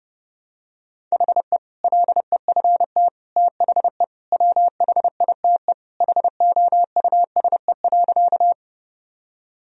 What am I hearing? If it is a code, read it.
Morse code, heard as a keyed tone, about 30 wpm, 700 Hz. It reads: HE LEFT THE WHITE HOUSE.
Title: Sentences from Top 1000 Words - Review Part 1 - 30wpm